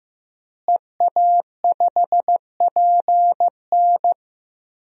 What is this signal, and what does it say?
Morse code, heard as a keyed tone, about 16 wpm, 700 Hz. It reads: EA5PN
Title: DXCC-100 Call Signs - 5 Characters - Repeated 2x - 15wpm